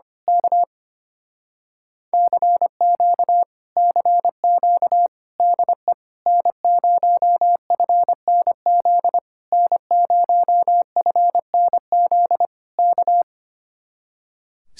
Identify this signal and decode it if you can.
Morse code, heard as a keyed tone, about 24 wpm, 700 Hz. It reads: K CQ CQ DE N0FNZ N0FNZ K
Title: Speed-Racing Common QSO Phrases - 25wpm